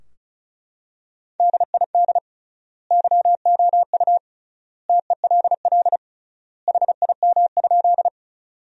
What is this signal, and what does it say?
Morse code, heard as a keyed tone, about 36 wpm, 700 Hz. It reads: DID YOU TELL HIM?
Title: 3x Word-Spacing - Sentences from Top 200 Words - 35wpm